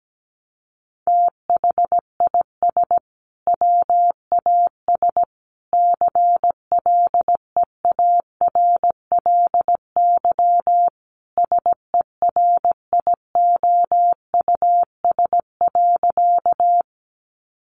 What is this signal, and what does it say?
Morse code, heard as a keyed tone, about 17 wpm, 700 Hz. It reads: THIS WAS CLEARLY SERIOUS.
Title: Sentences from Top 1300 Words 17wpm